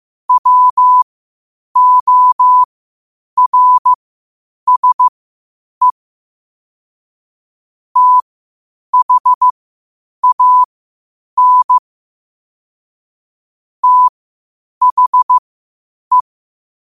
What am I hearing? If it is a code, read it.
Morse code, heard as a keyed tone, about 15 wpm, spaced as for 8 wpm, 1000 Hz. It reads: WORSE THAN THE
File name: qotdfiles / 8wpm